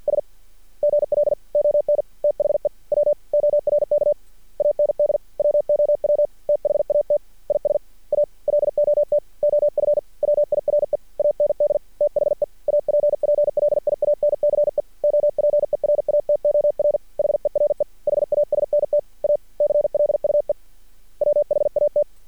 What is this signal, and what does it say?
Morse code, heard as a keyed tone, about 52 wpm, 590 Hz. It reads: S GL ON THE WORK AND WOW THAT IS A LOT OF WIRE AND THE APPLIANCE OPERATOR HERE HASNT A CLUE WHAT
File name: archived / CRSnet